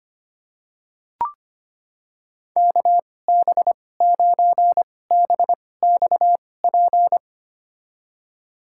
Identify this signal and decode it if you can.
Morse code, heard as a keyed tone, about 25 wpm, spaced as for 18 wpm, 700 Hz. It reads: KB9BXP
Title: USA Call Signs - 6 Characters - 2x Character Spacing - 25wpm